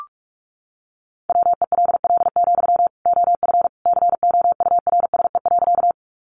Morse code, heard as a keyed tone, about 44 wpm, 700 Hz. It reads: WELL, OF COURSE.
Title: Sentences from Top 500 Words 45wpm